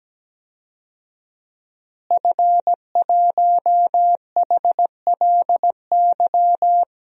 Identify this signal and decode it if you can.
Morse code, heard as a keyed tone, about 17 wpm, 700 Hz. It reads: F1HLY
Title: DXCC-100 Call Signs - 5 Characters - Repeated 3x - 17wpm